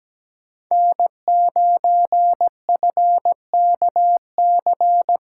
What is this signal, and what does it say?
Morse code, heard as a keyed tone, about 17 wpm, 700 Hz. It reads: N9FKC
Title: USA Callsign DE Callsign 17wpm